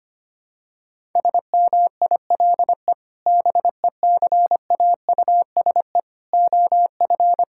Morse code, heard as a keyed tone, about 25 wpm, 700 Hz. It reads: SMILE BECAUSE OF
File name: practice / 25wpm / Sentences with 2-Word Phrases - 25wpm